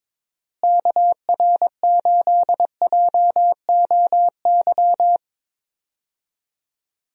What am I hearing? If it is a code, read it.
Morse code, heard as a keyed tone, about 22 wpm, 700 Hz. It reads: KR8JOY
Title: USA Call Signs - 6 Characters - Repeated 3x - 22wpm